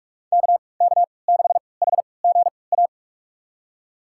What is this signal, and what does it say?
Morse code, heard as a keyed tone, about 45 wpm, spaced as for 25 wpm, 700 Hz. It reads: KK6HGA
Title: USA Call Signs - 3x Character Spacing - 45wpm